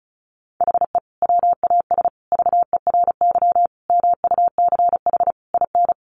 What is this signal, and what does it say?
Morse code, heard as a keyed tone, about 34 wpm, 700 Hz. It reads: HE WAS VERY MUCH IN
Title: Sentences from Top 1500 Words 35wpm